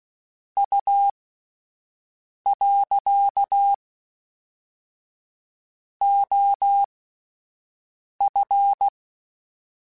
Morse code, heard as a keyed tone, about 16 wpm, spaced as for 5 wpm, 800 Hz. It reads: U. OF
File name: (spaced) / QSOs Spaced-mp3s / WA7QOX 1 @10 sp